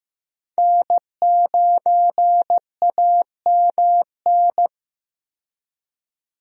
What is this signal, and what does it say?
Morse code, heard as a keyed tone, about 15 wpm, 700 Hz. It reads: N9AMN
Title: USA Call Signs - 5 Characters - 15wpm